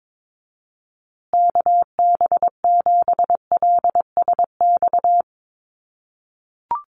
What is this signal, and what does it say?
Morse code, heard as a keyed tone, about 22 wpm, 700 Hz. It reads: KB7LSX
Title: USA Call Signs Repeated 2x 22wpm